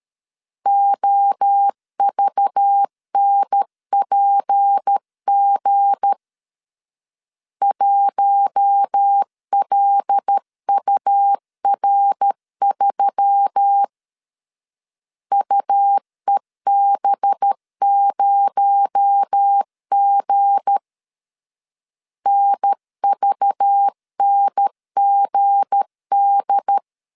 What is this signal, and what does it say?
Morse code, heard as a keyed tone, about 13 wpm, 800 Hz. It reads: OVNPG 1LUR3 UEB0G NVNGD
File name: LEZIONI / 7 GF9Z / 7 G 12wpm 2055